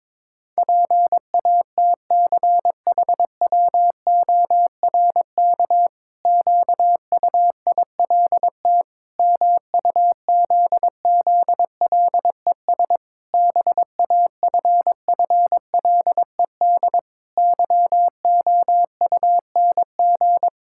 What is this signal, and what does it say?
Morse code, heard as a keyed tone, about 22 wpm, 700 Hz. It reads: PATCHWORK QUILT MUZZLES BAFFLED YOUNG
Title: Pangram Challenge v2 - 22wpm